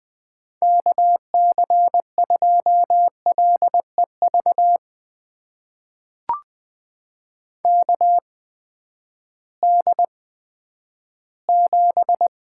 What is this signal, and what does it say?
Morse code, heard as a keyed tone, about 20 wpm, spaced as for 11 wpm, 700 Hz. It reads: KC2LEV KD7
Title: USA Call Signs - 6 Characters - 8x Character Spacing - 20wpm